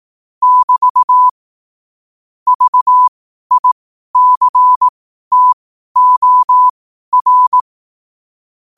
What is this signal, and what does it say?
Morse code, heard as a keyed tone, about 18 wpm, spaced as for 12 wpm, 1000 Hz. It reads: <BT> VICTOR
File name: qotdfiles / 12wpm